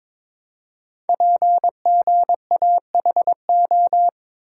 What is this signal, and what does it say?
Morse code, heard as a keyed tone, about 22 wpm, 700 Hz. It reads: PGAHO
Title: Code Groups - 5 Letters - 22wpm